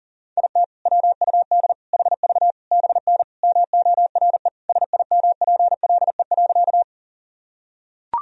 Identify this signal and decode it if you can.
Morse code, heard as a keyed tone, about 40 wpm, 700 Hz. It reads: IT WUD HV BN MORE SIMPLE.
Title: Sentences from Top 400 Words and Abbreviations 40wpm